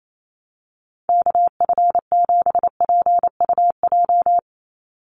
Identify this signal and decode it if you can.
Morse code, heard as a keyed tone, about 28 wpm, 700 Hz. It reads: KF7PUJ
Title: USA Call Signs 28wpm